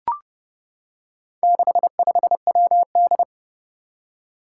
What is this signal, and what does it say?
Morse code, heard as a keyed tone, about 28 wpm, 700 Hz. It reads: 65WD
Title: Four Challenging Letter-Number Combinations 30wpm